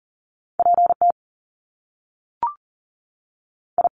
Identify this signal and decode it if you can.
Morse code, heard as a keyed tone, about 41 wpm, spaced as for 3 wpm, 700 Hz. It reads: P TI
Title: Top Two Letter Combinations in Words 40wpm